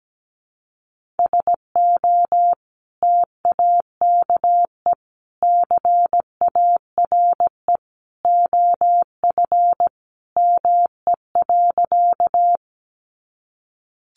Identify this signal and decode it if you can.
Morse code, heard as a keyed tone, about 17 wpm, 700 Hz. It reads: SO TAKE CARE OF ME.
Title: Sentences from Top 300 Words 17wpm